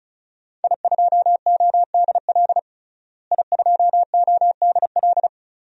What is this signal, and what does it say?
Morse code, heard as a keyed tone, about 34 wpm, 700 Hz. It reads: I2ODL I2ODL
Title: DXCC-100 Call Signs - 5 Characters - Repeated 3x - 35wpm